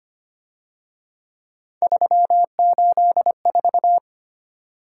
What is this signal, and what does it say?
Morse code, heard as a keyed tone, about 29 wpm, 700 Hz. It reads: 384
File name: Speed-Racing - Three Letter-Number Combinations - 25wpm